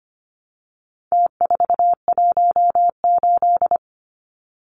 Morse code, heard as a keyed tone, about 25 wpm, 700 Hz. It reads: T418
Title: Numbers - Contest Style - 4-Digit 25wpm